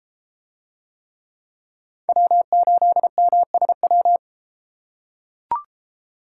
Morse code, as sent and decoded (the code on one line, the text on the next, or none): .-- ---.. -- ... .--
W8MSW